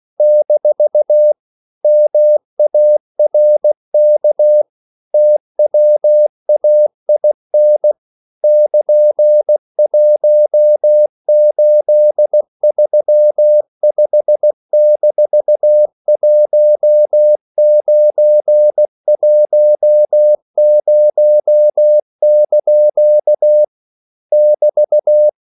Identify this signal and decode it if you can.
Morse code, heard as a keyed tone, about 16 wpm, 600 Hz. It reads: - MARK TWAIN <KN>1835-1910) <BT>
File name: CW / Practice / EnglishQuotations0416